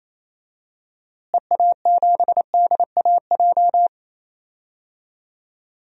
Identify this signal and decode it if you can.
Morse code, heard as a keyed tone, about 28 wpm, 700 Hz. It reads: EA7DAJ
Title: DXCC-100 Call Signs - 6 Characters - Repeated 2x - 28wpm